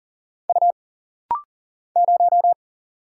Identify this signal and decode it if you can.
Morse code, heard as a keyed tone, about 40 wpm, 700 Hz. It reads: U 0
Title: Single Letter-Number Alt - Rapid-Fire 40wpm